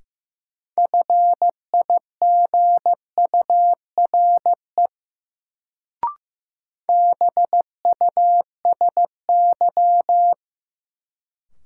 Morse code, heard as a keyed tone, about 16 wpm, 700 Hz. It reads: FIGURE BUSY
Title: Sentences from Top 1000 Words - Review Part 2 - New Word Drill - 15wpm